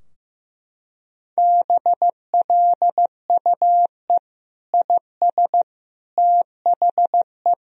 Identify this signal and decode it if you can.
Morse code, heard as a keyed tone, about 15 wpm, 700 Hz. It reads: BLUE IS THE
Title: Sentences from Top 1000 Words - Review Part 3 - 15wpm